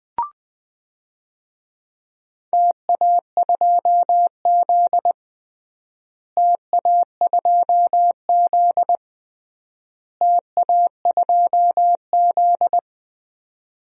Morse code, heard as a keyed tone, about 20 wpm, 700 Hz. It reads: TA2Z TA2Z TA2Z
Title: DXCC-100 Call Signs - 4 Characters - Repeated 3x - 20wpm